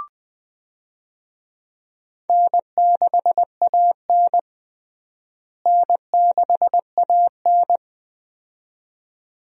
Morse code, heard as a keyed tone, about 21 wpm, 700 Hz. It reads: N6AN N6AN
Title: USA Call Signs - 4 Characters - Repeated 2x - 20wpm